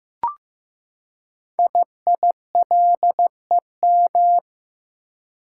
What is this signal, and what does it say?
Morse code, heard as a keyed tone, about 15 wpm, 700 Hz. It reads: IILEM
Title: Code Groups - 5 Letter:Numbers - 15wpm